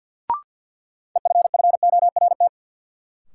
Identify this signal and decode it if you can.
Morse code, heard as a keyed tone, about 50 wpm, 700 Hz. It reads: EFFORT
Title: Sentences from Top 1200 Words - New Word Drill - 50wpm